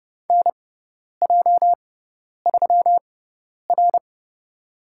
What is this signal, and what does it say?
Morse code, heard as a keyed tone, about 29 wpm, spaced as for 10 wpm, 700 Hz. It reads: NJ3R
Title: USA Call Signs - 4 Characters - 6x Character Spacing - 30wpm